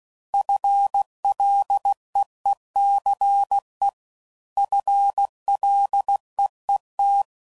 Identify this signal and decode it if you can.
Morse code, heard as a keyed tone, about 16 wpm, 800 Hz. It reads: FLEECE FLEET